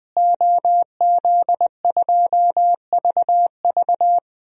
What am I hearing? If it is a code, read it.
Morse code, heard as a keyed tone, about 21 wpm, 700 Hz. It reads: OZ2VV
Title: DXCC-100 Call Signs - 5 Characters - Repeated 2x - 20wpm